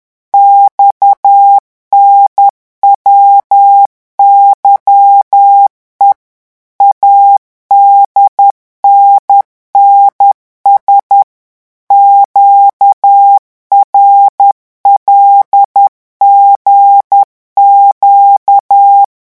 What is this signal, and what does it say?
Morse code, heard as a keyed tone, about 10 wpm, 800 Hz. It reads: XNWYE ADNNS QRLGQ
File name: morse mp3 / revision11 lettres vitesse 10